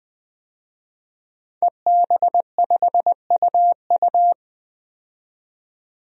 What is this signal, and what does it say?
Morse code, heard as a keyed tone, about 21 wpm, 700 Hz. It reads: EB5UU
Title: DXCC-100 Call Signs - 5 Characters - 20wpm